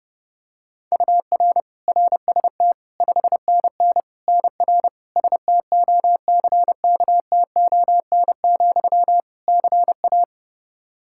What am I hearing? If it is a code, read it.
Morse code, heard as a keyed tone, about 30 wpm, 700 Hz. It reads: UR RST 5NN NR STOCKTON, CA
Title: Practice QSO - Exchange 1 - Beginning - 30wpm